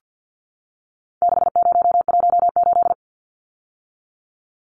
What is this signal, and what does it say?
Morse code, heard as a keyed tone, about 50 wpm, 700 Hz. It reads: 6018
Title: Numbers - 4-Digit 50wpm